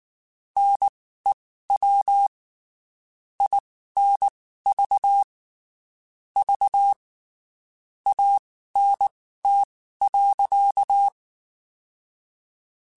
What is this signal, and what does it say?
Morse code, heard as a keyed tone, about 19 wpm, spaced as for 14 wpm, 800 Hz. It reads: NEW INV V ANT.
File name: QSOs Spaced